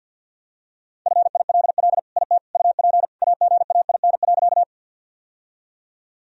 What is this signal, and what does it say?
Morse code, heard as a keyed tone, about 52 wpm, 700 Hz. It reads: FILL IT UP AGAIN.
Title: Sentences from Top 1000 Words - Review Part 3 - 50wpm